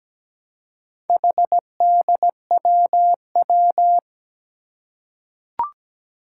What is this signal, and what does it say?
Morse code, heard as a keyed tone, about 17 wpm, 700 Hz. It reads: HDWW
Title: Four Challenging Letter-Number Combinations 17wpm